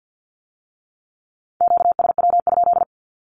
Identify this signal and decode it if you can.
Morse code, heard as a keyed tone, about 50 wpm, 700 Hz. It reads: QSW?
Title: Q-Codes with Spoken Meaning - 50wpm